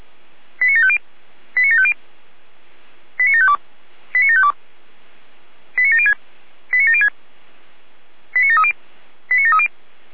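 Five-tone selective call (ZVEI), heard as quick runs of five tones, tones of about 80 ms. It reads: 89750 89750 89752 89752 89706 89706 89730 89730
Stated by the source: ZVEI five-tone call 89505 received on an SDR, from the feuerwehr-orlamuende.org fire brigade audio archive.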